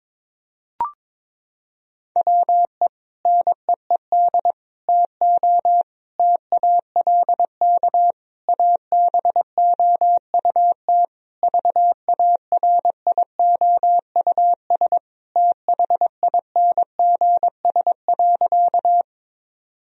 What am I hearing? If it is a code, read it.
Morse code, heard as a keyed tone, about 22 wpm, 700 Hz. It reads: WE NEED TO TALK ABOUT VARIOUS THINGS.
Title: Sentences from Top 1000 Words - Review Part 4 - 22wpm